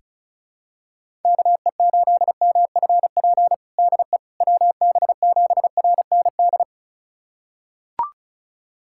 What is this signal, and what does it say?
Morse code, heard as a keyed tone, about 35 wpm, 700 Hz. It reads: KE8MFP DE WB7RND
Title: USA Callsign DE Callsign - 6 Characters - 35wpm